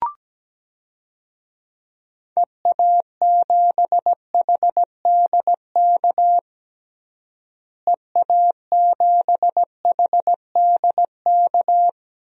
Morse code, heard as a keyed tone, about 17 wpm, 700 Hz. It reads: EA7HDK EA7HDK
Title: DXCC-100 Call Signs - 6 Characters - Repeated 3x - 17wpm